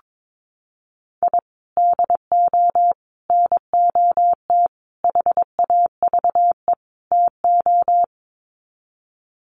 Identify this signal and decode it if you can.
Morse code, heard as a keyed tone, about 22 wpm, 700 Hz. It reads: I DO NOT HAVE TO
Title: Sets of 5 Words 22wpm